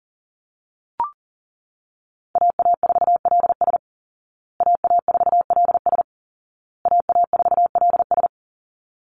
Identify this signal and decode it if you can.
Morse code, heard as a keyed tone, about 37 wpm, 700 Hz. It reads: AA4LS AA4LS AA4LS
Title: Call Signs Repeated 2x - 40wpm